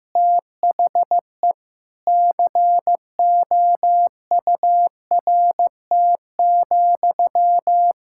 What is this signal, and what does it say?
Morse code, heard as a keyed tone, about 15 wpm, 700 Hz. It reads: THE COURT,
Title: Sentences from Top 1600 Words 15wpm